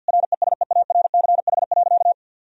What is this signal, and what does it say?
Morse code, heard as a keyed tone, about 50 wpm, 700 Hz. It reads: RESEARCH.
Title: Sentences from Top 1000 Words - Review Part 4 - 50wpm